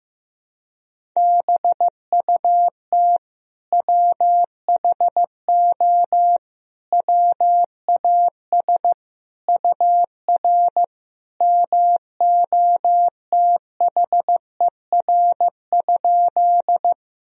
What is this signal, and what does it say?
Morse code, heard as a keyed tone, about 16 wpm, 700 Hz. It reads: BUT WHO WAS UR MOTHER?
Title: Sentences from Top 200 Words and Abbreviations 15wpm